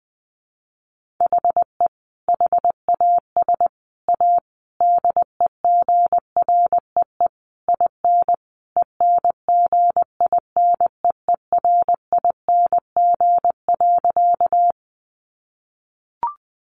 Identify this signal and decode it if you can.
Morse code, heard as a keyed tone, about 20 wpm, 700 Hz. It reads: HE HAS A DEGREE IN ENGINEERING.
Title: Sentences from Top 1500 Words 20wpm